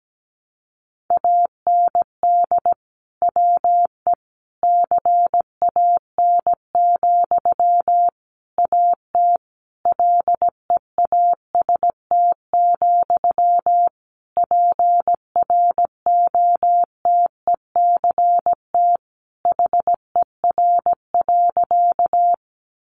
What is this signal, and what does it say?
Morse code, heard as a keyed tone, about 17 wpm, 700 Hz. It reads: AND WE CAN, AT LEAST, PROTECT HER.
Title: Sentences from Top 900 Words 17wpm